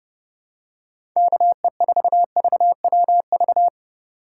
Morse code, heard as a keyed tone, about 31 wpm, 700 Hz. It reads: KE4VWV